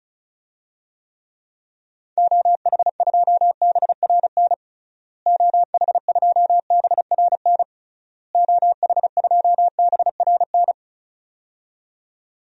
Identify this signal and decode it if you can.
Morse code, heard as a keyed tone, about 35 wpm, 700 Hz. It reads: OH2BRN OH2BRN OH2BRN
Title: DXCC-100 Call Signs - 6 Characters - Repeated 3x - 35wpm